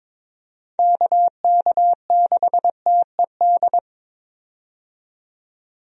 Morse code, heard as a keyed tone, about 22 wpm, 700 Hz. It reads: KK6TED
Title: USA Call Signs - 6 Characters - 22wpm